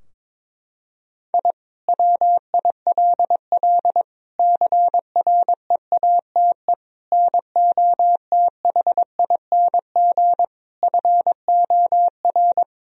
Morse code, heard as a keyed tone, about 21 wpm, 700 Hz. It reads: I WILL CREATE NOTHING FOR
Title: Sentences from Top 1000 Words - Review Part 3 - 22wpm